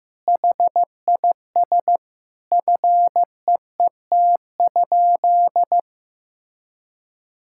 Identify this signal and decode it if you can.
Morse code, heard as a keyed tone, about 15 wpm, 700 Hz. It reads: HIS FEET?
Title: Sentences from Top 1000 Words - Review Part 6 - 15wpm